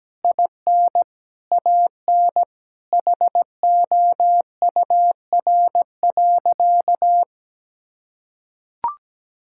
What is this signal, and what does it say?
Morse code, heard as a keyed tone, about 17 wpm, 700 Hz. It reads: IN AN HOUR.
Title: Sentences from Top 1000 Words - Review Part 5 - 17wpm